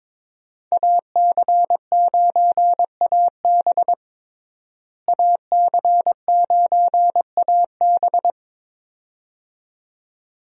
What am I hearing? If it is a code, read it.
Morse code, heard as a keyed tone, about 22 wpm, 700 Hz. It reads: AC9AB AC9AB
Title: USA Call Signs - 5 Characters - Repeated 2x - 22wpm